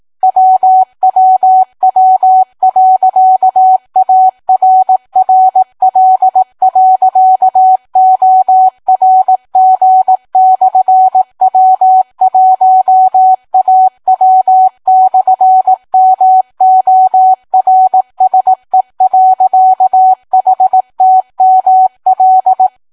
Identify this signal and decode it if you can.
Morse code, heard as a keyed tone, about 18 wpm, 750 Hz. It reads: WWW.ARRL.ORG/W1AW/MORSE.HTML